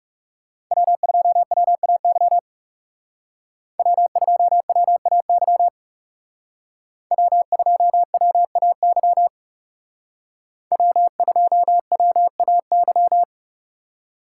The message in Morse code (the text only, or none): W2WAY W2WAY W2WAY W2WAY